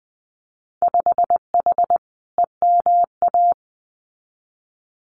Morse code, heard as a keyed tone, about 20 wpm, 700 Hz. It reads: 5H EMA
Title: Winter Field Day Exchanges Encore 20wpm